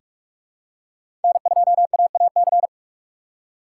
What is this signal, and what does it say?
Morse code, heard as a keyed tone, about 46 wpm, 700 Hz. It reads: N2AAC
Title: USA Call Signs - 5 Characters - 2x Character Spacing - 45wpm